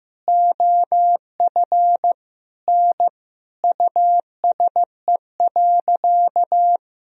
Morse code, heard as a keyed tone, about 15 wpm, 700 Hz. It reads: OF N USE.
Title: Sentences from Top 100 Words and Abbreviations 15wpm